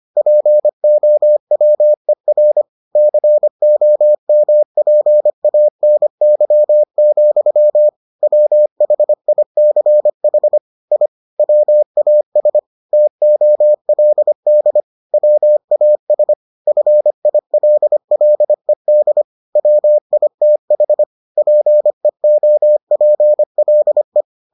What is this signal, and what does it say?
Morse code, heard as a keyed tone, about 25 wpm, 600 Hz. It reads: POWER COMPANY, WHICH I WAS TOLD WAS FILLED WITH PEOPLE